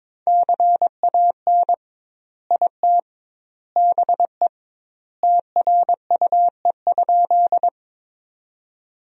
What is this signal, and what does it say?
Morse code, heard as a keyed tone, about 22 wpm, 700 Hz. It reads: CAN IT BE TRUE?
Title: 2x Word-Spacing - Sentences from Top 400 Words - 22wpm